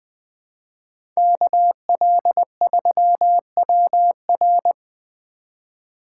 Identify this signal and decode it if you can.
Morse code, heard as a keyed tone, about 20 wpm, 700 Hz. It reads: KL3WR
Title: USA Call Signs - 4x Character Spacing - 20wpm